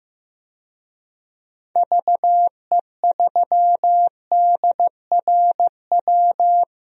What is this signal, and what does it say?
Morse code, heard as a keyed tone, about 15 wpm, 700 Hz. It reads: VE3DRW
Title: DXCC-100 Call Signs - 6 Characters - Repeated 3x - 15wpm